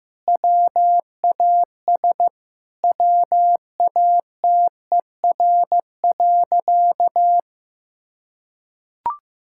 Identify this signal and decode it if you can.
Morse code, heard as a keyed tone, about 15 wpm, 700 Hz. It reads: WAS WATER.
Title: Sentences from Top 1600 Words 15wpm